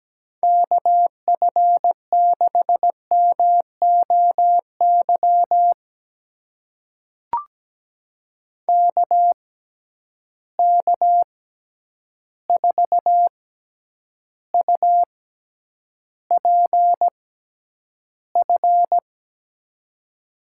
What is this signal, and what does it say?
Morse code, heard as a keyed tone, about 17 wpm, spaced as for 9 wpm, 700 Hz. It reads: KF6MOY KK4UPF
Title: USA Call Signs - 6 Characters - 6x Character Spacing - 17wpm